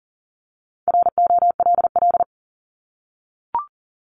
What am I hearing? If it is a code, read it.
Morse code, heard as a keyed tone, about 38 wpm, 700 Hz. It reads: ROLL
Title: Multi-letter Phonemes 40wpm